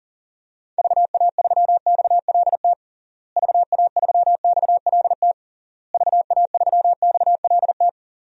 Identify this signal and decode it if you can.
Morse code, heard as a keyed tone, about 37 wpm, 700 Hz. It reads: VA3XLT VA3XLT VA3XLT
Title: DXCC-100 Call Signs - 6 Characters - Repeated 3x - 40wpm